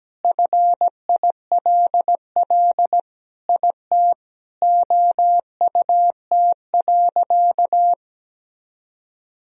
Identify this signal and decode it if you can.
Morse code, heard as a keyed tone, about 17 wpm, 700 Hz. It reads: FILL IT OUT.
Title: Sentences from Top 1000 Words - Review Part 3 - 17wpm